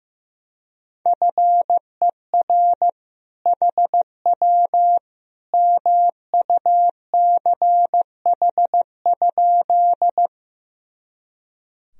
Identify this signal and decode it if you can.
Morse code, heard as a keyed tone, about 15 wpm, 700 Hz. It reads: FER HW MUCH?
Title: Sentences from Top 200 Words and Abbreviations 15wpm